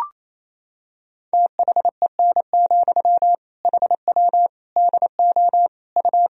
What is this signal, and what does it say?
Morse code, heard as a keyed tone, about 28 wpm, 700 Hz. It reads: THEN, HW DO U
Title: Sentences from Top 100 Words and Abbreviations 28wpm